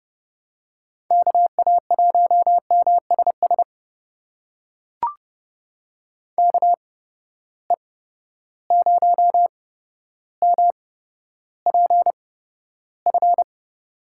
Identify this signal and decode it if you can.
Morse code, heard as a keyed tone, about 30 wpm, spaced as for 13 wpm, 700 Hz. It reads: KA1MSS KE0MPF
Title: USA Call Signs - 6 Characters - 8x Character Spacing - 30wpm